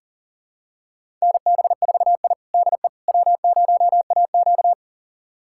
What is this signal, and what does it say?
Morse code, heard as a keyed tone, about 40 wpm, 700 Hz. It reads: NB4I DE W0AQ
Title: USA Callsign DE Callsign - 4 Characters - 40wpm